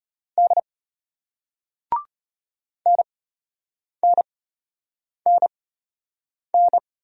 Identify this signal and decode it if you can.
Morse code, heard as a keyed tone, about 33 wpm, spaced as for 8 wpm, 700 Hz. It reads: D NNNN